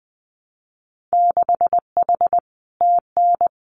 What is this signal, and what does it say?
Morse code, heard as a keyed tone, about 19 wpm, 700 Hz. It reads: 6H TN
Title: Winter Field Day Exchanges Encore 20wpm